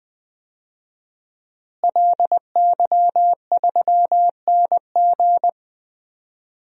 Morse code, heard as a keyed tone, about 20 wpm, 700 Hz. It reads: LY3NG
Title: DXCC-100 Call Signs - 5 Characters - Repeated 2x - 20wpm